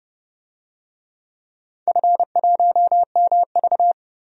Morse code, heard as a keyed tone, about 30 wpm, 700 Hz. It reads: F1MV
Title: DXCC-100 Call Signs - 4 Characters - Repeated 2x - 30wpm